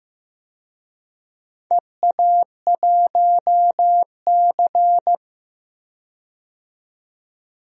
Morse code, heard as a keyed tone, about 15 wpm, 700 Hz. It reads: EA1C
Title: DXCC-100 Call Signs - 4 Characters - 15wpm